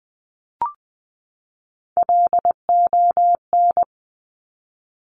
Morse code, heard as a keyed tone, about 20 wpm, 700 Hz. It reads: LON
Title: Top Three Letter Combinations in Words 20wpm